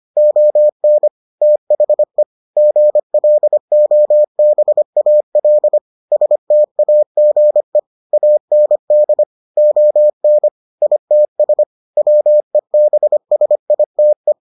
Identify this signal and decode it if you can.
Morse code, heard as a keyed tone, about 25 wpm, 600 Hz. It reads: ON THE GLOBAL STAGE AND ON ITS WEBSITE